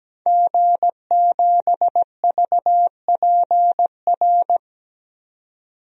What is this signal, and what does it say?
Morse code, heard as a keyed tone, about 17 wpm, 700 Hz. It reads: G7VPR
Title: DXCC-100 Call Signs - 5 Characters - 17wpm